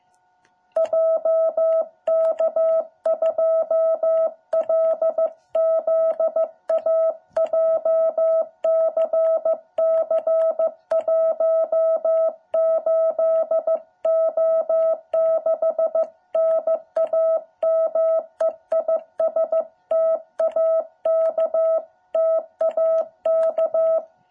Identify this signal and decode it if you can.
Morse code, heard as a keyed tone, about 15 wpm, 660 Hz. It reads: JK2LZA JCC18O6 NAME IS TAK TAK
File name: jh8cbh / 2021cw